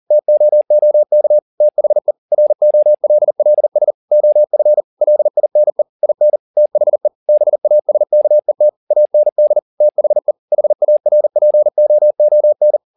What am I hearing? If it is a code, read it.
Morse code, heard as a keyed tone, about 40 wpm, 600 Hz. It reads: TOOK THE ROLLS OF LINE IN THE BASKET AND THE HARPOON